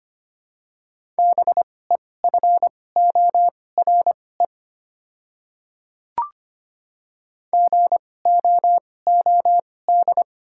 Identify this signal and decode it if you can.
Morse code, heard as a keyed tone, about 25 wpm, spaced as for 18 wpm, 700 Hz. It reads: BEFORE GOOD
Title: 2x Character Spacing - Top 200 Words - 25wpm